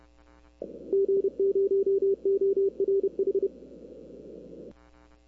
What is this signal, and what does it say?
Morse code, heard as a keyed tone, about 30 wpm, 390 Hz. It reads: G0ORH
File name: arrl cw dx